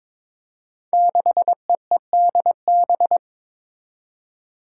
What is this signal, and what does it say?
Morse code, heard as a keyed tone, about 22 wpm, 700 Hz. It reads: 6EEDB
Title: Five Challenging Letter-Number Combinations 22wpm